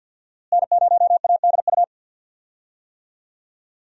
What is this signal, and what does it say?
Morse code, heard as a keyed tone, about 51 wpm, 700 Hz. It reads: N0ADU
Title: USA Call Signs - 5 Characters - Repeated 2x - 50wpm